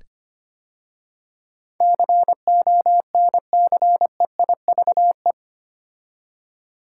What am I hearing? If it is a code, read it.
Morse code, heard as a keyed tone, about 25 wpm, 700 Hz. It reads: CONCEIVE
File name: 8 Letter Words - First 3 Letters Given - 25wpm